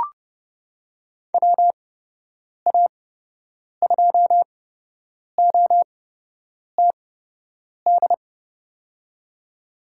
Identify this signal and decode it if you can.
Morse code, heard as a keyed tone, about 30 wpm, spaced as for 8 wpm, 700 Hz. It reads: WA2OTD
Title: USA Call Signs - 8x Character Spacing - 30wpm